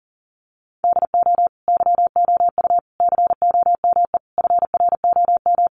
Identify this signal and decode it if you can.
Morse code, heard as a keyed tone, about 40 wpm, 700 Hz. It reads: DO YOU COME FROM
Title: Sentences from Top 200 Words 40wpm